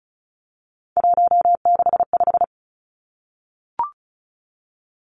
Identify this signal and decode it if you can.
Morse code, heard as a keyed tone, about 36 wpm, 700 Hz. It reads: 165
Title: Three Challenging Letter-Number Combinations 35wpm